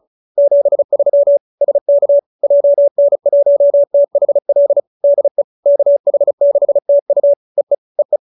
Morse code, heard as a keyed tone, about 34 wpm, 570 Hz. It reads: 73 SK JN1THL DE KH6TU EE EE